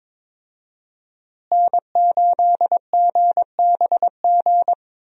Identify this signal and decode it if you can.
Morse code, heard as a keyed tone, about 22 wpm, 700 Hz. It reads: N8GBG
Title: USA Call Signs - 5 Characters - Repeated 3x - 22wpm